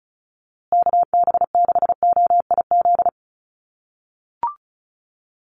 Morse code, heard as a keyed tone, about 34 wpm, 700 Hz. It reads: KB6OIZ
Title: USA Call Signs Repeated 2x 35wpm